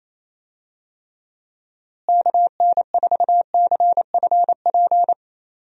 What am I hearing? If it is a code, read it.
Morse code, heard as a keyed tone, about 28 wpm, 700 Hz. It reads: KN4CFP
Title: USA Call Signs - 6 Characters - Repeated 3x - 28wpm